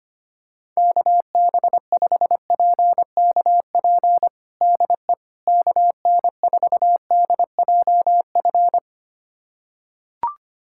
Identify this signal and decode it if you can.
Morse code, heard as a keyed tone, about 25 wpm, 700 Hz. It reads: KB5PKP DE KN4DJF
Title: USA Callsign DE Callsign - 6 Characters - 25wpm